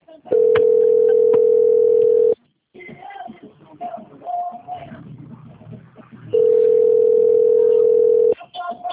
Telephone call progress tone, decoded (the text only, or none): ringback tone